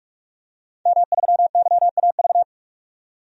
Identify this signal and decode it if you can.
Morse code, heard as a keyed tone, about 46 wpm, 700 Hz. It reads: M3YAV
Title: DXCC-100 Call Signs - 5 Characters - 45wpm